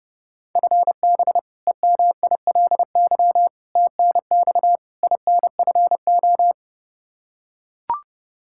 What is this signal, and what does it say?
Morse code, heard as a keyed tone, about 30 wpm, 700 Hz. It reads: FB EMILY TNX INFO
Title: Practice QSO - Exchange 3 - More Info - 30wpm